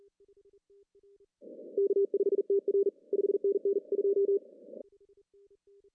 Morse code, heard as a keyed tone, about 39 wpm, 370 Hz. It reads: K5TR 5NN 2
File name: CW / cqww-cw-k5tr